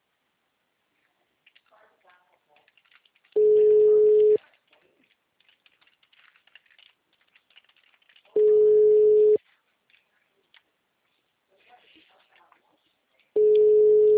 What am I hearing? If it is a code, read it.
Ringback tone (Europe, 425 Hz), with speech.